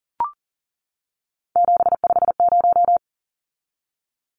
Morse code, heard as a keyed tone, about 40 wpm, 700 Hz. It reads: 750
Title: Numbers - 3-Digit 40wpm